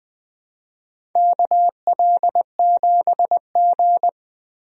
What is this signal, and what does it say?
Morse code, heard as a keyed tone, about 21 wpm, 700 Hz. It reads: KL7G
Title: DXCC-100 Call Signs - 4 Characters - Repeated 3x - 20wpm